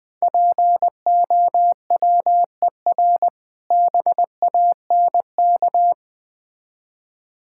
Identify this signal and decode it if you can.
Morse code, heard as a keyed tone, about 20 wpm, 700 Hz. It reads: POWER BANK
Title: Difficult Words with Hints - 20wpm